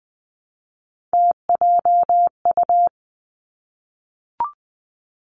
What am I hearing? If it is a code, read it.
Morse code, heard as a keyed tone, about 20 wpm, 700 Hz. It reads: TJU
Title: Three Challenging Letter-Number Combinations 20wpm